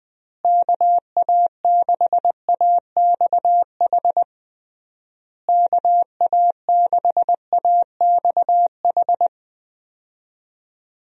Morse code, heard as a keyed tone, about 21 wpm, 700 Hz. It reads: KA6AXH KA6AXH
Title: USA Call Signs - 6 Characters - Repeated 3x - 20wpm